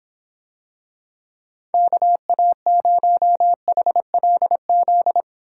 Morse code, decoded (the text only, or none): KA0HLZ